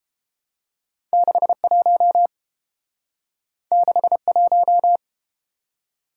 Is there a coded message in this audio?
Yes: Morse code from 1.1 to 5.0 s, heard as a keyed tone, about 30 wpm, 700 Hz. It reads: 61 61